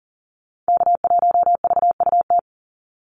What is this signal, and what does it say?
Morse code, heard as a keyed tone, about 40 wpm, 700 Hz. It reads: K1VUT
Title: Call Signs Repeated 2x - 40wpm